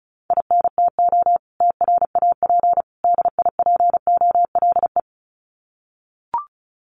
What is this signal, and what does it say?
Morse code, heard as a keyed tone, about 34 wpm, 700 Hz. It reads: INTO TRAP DIPOLE
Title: Common QSO Phrases 35wpm